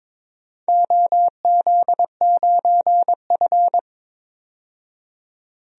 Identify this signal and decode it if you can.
Morse code, heard as a keyed tone, about 22 wpm, 700 Hz. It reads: OZ9F
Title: DXCC-100 Call Signs - 4 Characters - 22wpm